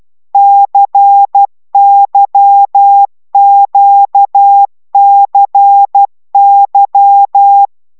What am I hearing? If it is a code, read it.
Morse code, heard as a keyed tone, about 12 wpm, 800 Hz. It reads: CYQCY
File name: taviro tanulas / cw 75 15 10